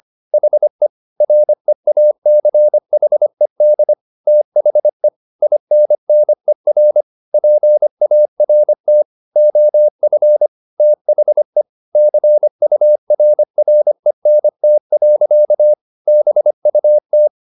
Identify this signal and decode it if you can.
Morse code, heard as a keyed tone, about 25 wpm, 600 Hz. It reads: HE REACHED THE INNER PART OF THE CURRENT. BUT